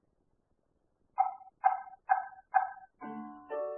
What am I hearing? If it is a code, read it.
There is no code.